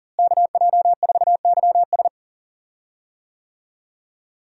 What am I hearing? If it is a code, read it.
Morse code, heard as a keyed tone, about 40 wpm, 700 Hz. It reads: KJ4YS
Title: USA Call Signs - 5 Characters - Repeated 3x - 40wpm